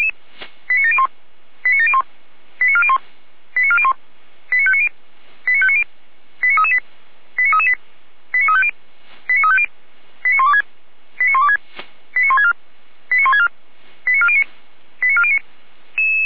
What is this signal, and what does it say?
Five-tone selective call (ZVEI), heard as quick runs of five tones, tones of about 70 ms. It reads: 89701 89701 89401 89401 89590 89590 89338 89338 89350 89350 89126 89126 89164 89164 89409 89409